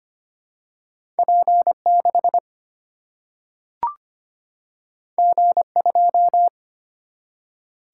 Morse code, heard as a keyed tone, about 25 wpm, 700 Hz. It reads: P6 G2
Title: Single-Letter Single-Number 25wpm